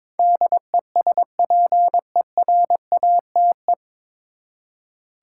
Morse code, heard as a keyed tone, about 21 wpm, 700 Hz. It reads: DESPERATE
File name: 9 Letter Words - First 3 Letters Given - 22wpm